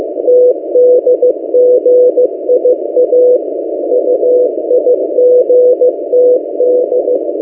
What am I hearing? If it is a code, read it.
Morse code, heard as a keyed tone, about 15 wpm, 500 Hz. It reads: TDGIA UIGTD
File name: M08a- -1355utc-8096khz